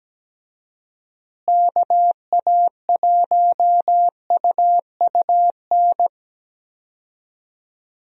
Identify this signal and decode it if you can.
Morse code, heard as a keyed tone, about 17 wpm, 700 Hz. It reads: KA1UUN
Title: USA Call Signs - 6 Characters - Repeated 2x - 17wpm